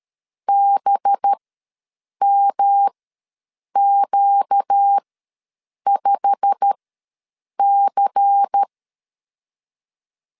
Morse code, heard as a keyed tone, about 13 wpm, spaced as for 7 wpm, 800 Hz. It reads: BMQ5C
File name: LEZIONI / 10 SIGNS / 10D COMA ws 12wpm 3618